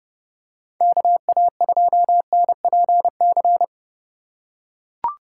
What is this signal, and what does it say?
Morse code, heard as a keyed tone, about 30 wpm, 700 Hz. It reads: KA2NPC
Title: USA Call Signs - 6 Characters - 3x Character Spacing - 30wpm